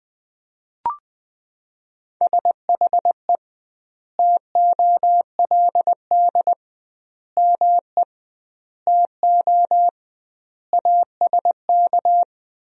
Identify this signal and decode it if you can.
Morse code, heard as a keyed tone, about 21 wpm, 700 Hz. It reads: SHE TOLD ME TO ASK
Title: 2x Word-Spacing - Sentences from Top 400 Words - 20wpm